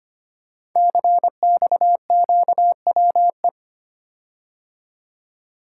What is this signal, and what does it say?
Morse code, heard as a keyed tone, about 25 wpm, 700 Hz. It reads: CXQWE
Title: Code Groups - 5 Letters - 25wpm